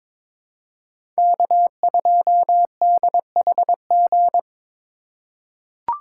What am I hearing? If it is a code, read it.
Morse code, heard as a keyed tone, about 22 wpm, 700 Hz. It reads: K2DHG
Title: USA Call Signs - 5 Characters - 2x Character Spacing - 22wpm